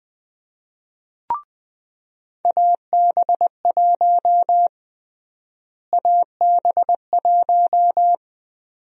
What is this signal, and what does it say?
Morse code, heard as a keyed tone, about 20 wpm, 700 Hz. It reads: AB1 AB1
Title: USA Call Sign Prefixes - Repeated 2x - 20wpm